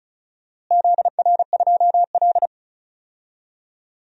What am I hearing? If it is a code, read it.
Morse code, heard as a keyed tone, about 34 wpm, 700 Hz. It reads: ZR2L
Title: DXCC-100 Call Signs - 4 Characters - Repeated 3x - 35wpm